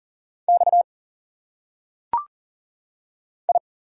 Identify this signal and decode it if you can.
Morse code, heard as a keyed tone, about 40 wpm, 700 Hz. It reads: X I
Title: Single Letters - Phonetic Pronunciation 40wpm